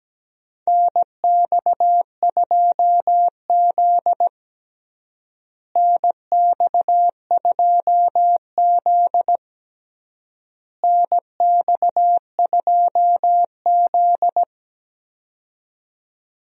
Morse code, heard as a keyed tone, about 17 wpm, 700 Hz. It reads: NX2Z NX2Z NX2Z
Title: USA Call Signs - 4 Characters - Repeated 3x - 17wpm